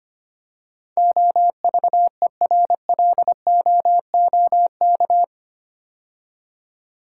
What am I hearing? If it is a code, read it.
Morse code, heard as a keyed tone, about 25 wpm, 700 Hz. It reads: OVERLOOK
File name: 8 Letter Words - First 3 Letters Given - 25wpm